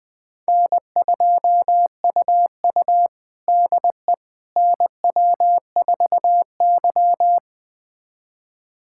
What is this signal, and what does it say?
Morse code, heard as a keyed tone, about 21 wpm, 700 Hz. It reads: N2UU DE NW4Y
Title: USA Callsign DE Callsign - 4 Characters - 20wpm